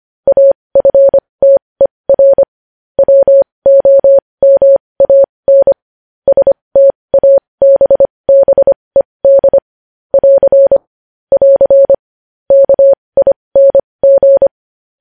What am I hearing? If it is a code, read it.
Morse code, heard as a keyed tone, about 25 wpm, spaced as for 20 wpm, 550 Hz. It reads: AFTER WOMAN STABBED <AR> <AR> KING
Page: news-headlines-in-morse-code-at-20-wpm